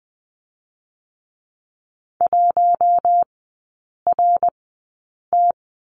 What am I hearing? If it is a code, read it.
Morse code, heard as a keyed tone, about 20 wpm, spaced as for 8 wpm, 700 Hz. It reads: 1RT
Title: 053 Lesson 10e - Letter r - Three Characters 20wpm